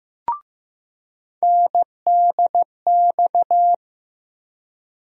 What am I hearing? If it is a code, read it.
Morse code, heard as a keyed tone, about 15 wpm, 700 Hz. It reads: NDX
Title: USA Call Sign Postfixes - 3 Letters 15wpm